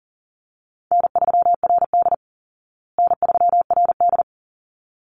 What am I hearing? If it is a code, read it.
Morse code, heard as a keyed tone, about 40 wpm, 700 Hz. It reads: N3RD N3RD
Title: Call Signs Repeated 3x - 40wpm